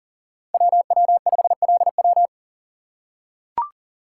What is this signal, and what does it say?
Morse code, heard as a keyed tone, about 40 wpm, 700 Hz. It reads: WW5LW
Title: USA Call Signs - 3x Character Spacing - 40wpm